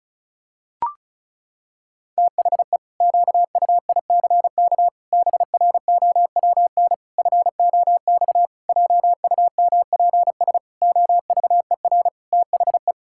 Morse code, heard as a keyed tone, about 36 wpm, 700 Hz. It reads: THE QUICK BROWN FOX JUMPS OVER THE
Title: Pangram Challenge v2 - 35wpm